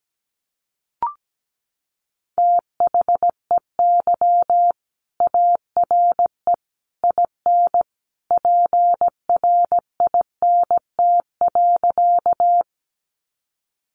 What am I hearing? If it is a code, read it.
Morse code, heard as a keyed tone, about 17 wpm, 700 Hz. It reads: THEY ARE IN PRINT.
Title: Sentences from Top 1400 Words 17wpm